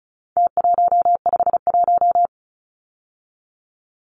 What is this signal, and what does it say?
Morse code, heard as a keyed tone, about 34 wpm, 700 Hz. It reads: T151
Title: Numbers - Contest Style - 4-Digit 35wpm